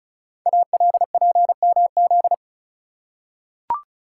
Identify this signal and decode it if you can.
Morse code, heard as a keyed tone, about 36 wpm, 700 Hz. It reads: ALPMZ
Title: Code Groups - 5 Letter:Numbers - 35wpm